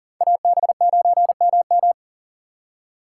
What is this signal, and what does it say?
Morse code, heard as a keyed tone, about 50 wpm, 700 Hz. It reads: AB9MM
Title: Speed Racing - USA Call Signs 40wpm